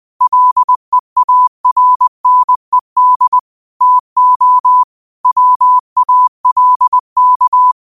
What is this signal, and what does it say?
Morse code, heard as a keyed tone, about 20 wpm, 1000 Hz. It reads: LEARNED TO WALK
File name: qotdfiles / 20wpm